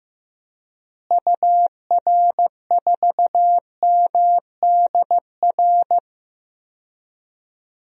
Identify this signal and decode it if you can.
Morse code, heard as a keyed tone, about 15 wpm, 700 Hz. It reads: UR4MDR
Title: DXCC-100 Call Signs - 6 Characters - Repeated 3x - 15wpm